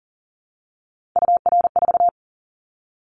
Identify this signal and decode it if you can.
Morse code, heard as a keyed tone, about 40 wpm, 700 Hz. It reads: UR4
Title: Callsign Prefixes for Top 25 DXCC - Repeated 3x - 40wpm